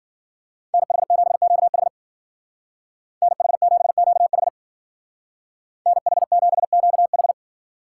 Morse code, heard as a keyed tone, about 48 wpm, 700 Hz. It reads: NH7QH NH7QH NH7QH
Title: Speed Racing - USA 2x2 Call Signs 40wpm